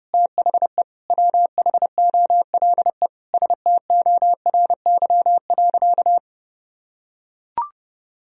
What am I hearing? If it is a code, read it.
Morse code, heard as a keyed tone, about 30 wpm, 700 Hz. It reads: THE WHOLE STORY.
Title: Sentences from Top 1000 Words - Review Part 1 - 30wpm